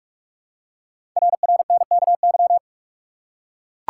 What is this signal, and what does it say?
Morse code, heard as a keyed tone, about 44 wpm, 700 Hz. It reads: RRNKY
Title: Code Groups - 5 Letters - 45wpm